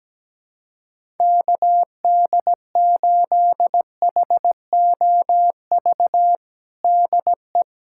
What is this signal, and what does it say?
Morse code, heard as a keyed tone, about 17 wpm, 700 Hz. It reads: KD8HOV DE
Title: USA Callsign DE Callsign - 6 Characters - 17wpm